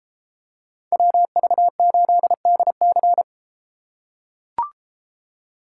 Morse code, heard as a keyed tone, about 33 wpm, 700 Hz. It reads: WV8DC